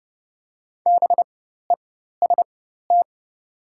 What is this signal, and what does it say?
Morse code, heard as a keyed tone, about 30 wpm, spaced as for 14 wpm, 700 Hz. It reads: BEST
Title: 4x Character Spacing - Top 500 Words - 30wpm